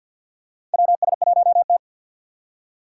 Morse code, heard as a keyed tone, about 50 wpm, 700 Hz. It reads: WS1T
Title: USA Call Signs - 2x Character Spacing - 50wpm